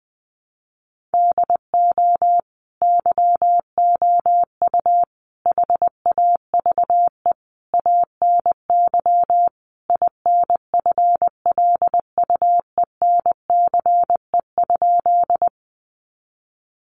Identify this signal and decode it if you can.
Morse code, heard as a keyed tone, about 19 wpm, 700 Hz. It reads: DO YOU HAVE ANY INFLUENCE?
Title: Sentences from Top 1100 Words 20wpm